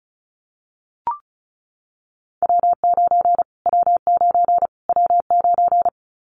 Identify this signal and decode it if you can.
Morse code, heard as a keyed tone, about 34 wpm, 700 Hz. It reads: W9 W9 W9
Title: Callsign Prefixes for Top 25 DXCC - Repeated 3x - 35wpm